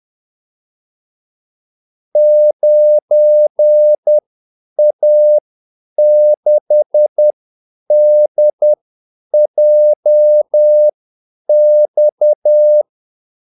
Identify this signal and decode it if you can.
Morse code, heard as a keyed tone, about 10 wpm, spaced as for 8 wpm, 600 Hz. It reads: 9A6DJX
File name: Callsigns-10-8wpm